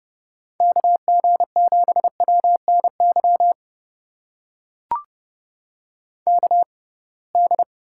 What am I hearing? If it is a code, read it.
Morse code, heard as a keyed tone, about 30 wpm, spaced as for 23 wpm, 700 Hz. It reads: KG7WNY KD
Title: USA Call Signs - 6x Character Spacing - 30wpm